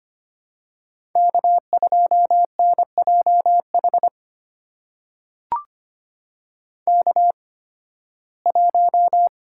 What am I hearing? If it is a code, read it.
Morse code, heard as a keyed tone, about 25 wpm, spaced as for 16 wpm, 700 Hz. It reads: K2NJH K1
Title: USA Call Signs - 5 Characters - 8x Character Spacing - 25wpm